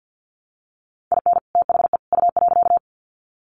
Morse code, heard as a keyed tone, about 50 wpm, 700 Hz. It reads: IN THE U.
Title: Sets of 3 Words 50wpm